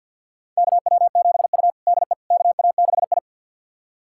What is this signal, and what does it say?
Morse code, heard as a keyed tone, about 52 wpm, 700 Hz. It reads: KW7U DE KA6I
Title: USA Callsign DE Callsign - 4 Characters - 50wpm